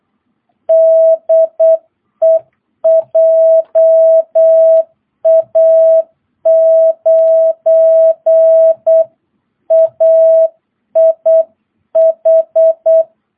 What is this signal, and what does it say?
Morse code, heard as a keyed tone, about 8 wpm, 660 Hz. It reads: DE JA9AIH
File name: jh8cbh / 2022cw